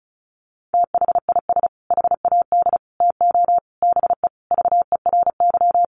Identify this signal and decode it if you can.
Morse code, heard as a keyed tone, about 34 wpm, 700 Hz. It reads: THIS HAD TO BE VERY
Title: Sentences from Top 1300 Words 35wpm